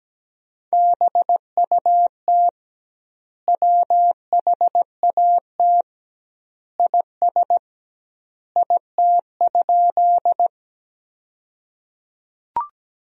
Morse code, heard as a keyed tone, about 17 wpm, 700 Hz. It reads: BUT WHAT IS IT?
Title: 2x Word-Spacing - Sentences from Top 100 Words - 17wpm